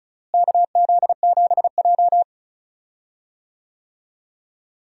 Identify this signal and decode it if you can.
Morse code, heard as a keyed tone, about 34 wpm, 700 Hz. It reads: KZ7J
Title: USA Call Signs - 4 Characters - Repeated 3x - 35wpm